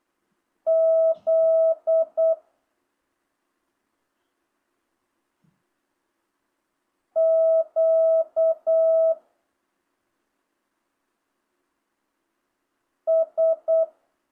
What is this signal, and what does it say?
Morse code, heard as a keyed tone, about 8 wpm, 650 Hz. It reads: Z Q S